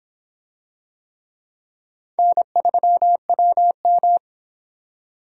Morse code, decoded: N3WM

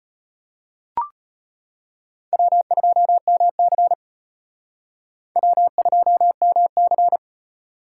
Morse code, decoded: W2MC W2MC

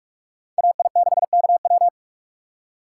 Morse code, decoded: AI6KW

